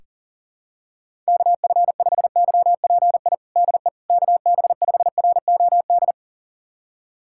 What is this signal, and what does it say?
Morse code, heard as a keyed tone, about 40 wpm, 700 Hz. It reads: KF5YPI DE KB5ROD
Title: USA Callsign DE Callsign 40wpm